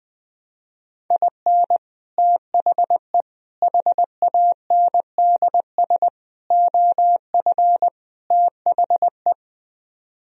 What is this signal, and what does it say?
Morse code, heard as a keyed tone, about 20 wpm, 700 Hz. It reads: IN THE HANDS OF THE
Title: Speed-Racing - Sets of 5 Words 20wpm